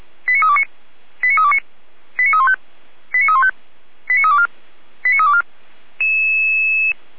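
Five-tone selective call (ZVEI), heard as quick runs of five tones, tones of about 80 ms. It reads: 89329 89329 89315 89315 89324 89324
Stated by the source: Five-tone ZVEI selective call received on an SDR.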